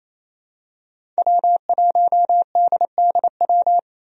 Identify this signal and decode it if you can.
Morse code, heard as a keyed tone, about 28 wpm, 700 Hz. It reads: W1DDW
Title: USA Call Signs - 5 Characters - Repeated 2x - 28wpm